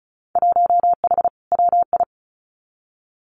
Morse code, heard as a keyed tone, about 34 wpm, 700 Hz. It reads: 1H WI
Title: Winter Field Day Exchanges 35wpm